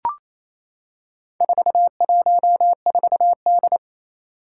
Morse code, heard as a keyed tone, about 27 wpm, 700 Hz. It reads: 414D
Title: Four Challenging Letter-Number Combinations 28wpm